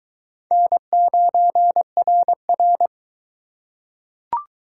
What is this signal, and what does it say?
Morse code, heard as a keyed tone, about 22 wpm, 700 Hz. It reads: N9RR